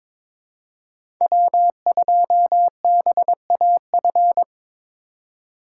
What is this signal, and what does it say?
Morse code, heard as a keyed tone, about 22 wpm, 700 Hz. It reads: W2BAF
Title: USA Call Signs - 3x Character Spacing - 22wpm